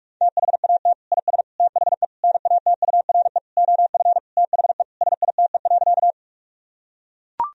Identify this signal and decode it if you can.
Morse code, heard as a keyed tone, about 41 wpm, 700 Hz. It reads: THAT IS THE NATURE OF THE SITE.
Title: Sentences from Top 1000 Words - Review Part 4 - 45wpm